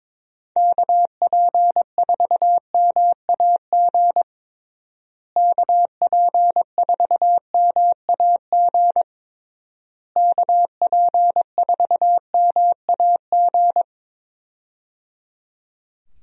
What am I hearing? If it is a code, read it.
Morse code, heard as a keyed tone, about 23 wpm, 700 Hz. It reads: KP4MAG KP4MAG KP4MAG
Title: USA Call Signs - 6 Characters - Repeated 3x - 22wpm